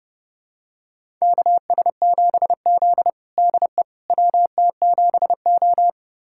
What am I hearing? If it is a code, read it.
Morse code, heard as a keyed tone, about 30 wpm, 700 Hz. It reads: KS7Z DE WT7O
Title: USA Callsign DE Callsign - 4 Characters - 30wpm